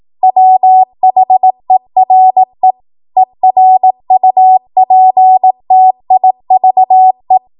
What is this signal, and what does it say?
Morse code, heard as a keyed tone, about 18 wpm, 750 Hz. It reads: WHERE ERUPTIVE
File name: Morse / 18WPM